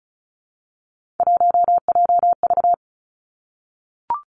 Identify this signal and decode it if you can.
Morse code, heard as a keyed tone, about 34 wpm, 700 Hz. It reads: 1JV
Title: Three Challenging Letter-Number Combinations 35wpm